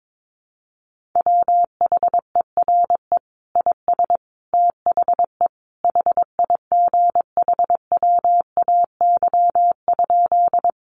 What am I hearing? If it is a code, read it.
Morse code, heard as a keyed tone, about 22 wpm, 700 Hz. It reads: WHERE IS THE HIGHWAY?
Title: Sentences from Top 1400 Words 22wpm